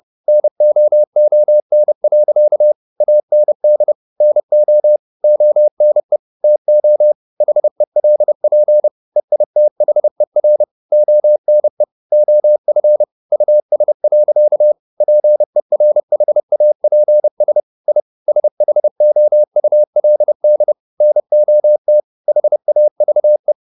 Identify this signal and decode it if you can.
Morse code, heard as a keyed tone, about 30 wpm, 600 Hz. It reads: NOON. AND NO ONE TO HELP EITHER ONE OF US. PERHAPS I SHOULD NOT HAVE
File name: the old man and the sea / the old man and the sea 1 30wpm